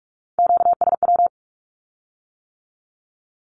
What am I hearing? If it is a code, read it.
Morse code, heard as a keyed tone, about 46 wpm, 700 Hz. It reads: QSW
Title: Q-Codes with Spoken Meaning - 45wpm